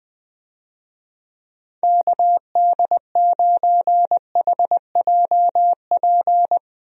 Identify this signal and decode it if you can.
Morse code, heard as a keyed tone, about 20 wpm, 700 Hz. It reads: KD9HJP
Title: USA Call Signs - 6 Characters - Repeated 2x - 20wpm